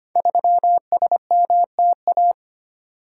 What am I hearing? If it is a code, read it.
Morse code, heard as a keyed tone, about 25 wpm, 700 Hz. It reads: 3SMTA
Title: Code Groups - 5 Letter:Numbers - 25wpm